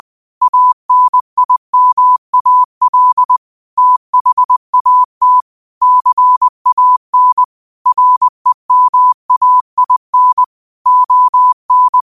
Morse code, heard as a keyed tone, about 20 wpm, 1000 Hz. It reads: ANIMAL THAT CAN REMAIN ON
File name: qotdfiles / 20wpm